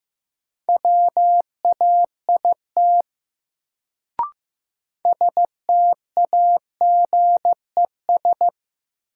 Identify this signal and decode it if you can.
Morse code, heard as a keyed tone, about 15 wpm, 700 Hz. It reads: WAIT STAGES
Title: Sets of 5 Words Encore - New Word Drill - 15wpm